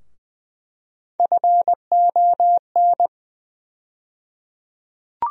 Morse code, heard as a keyed tone, about 20 wpm, 700 Hz. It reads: FON
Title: USA Call Sign Postfixes - 3 Letters 20wpm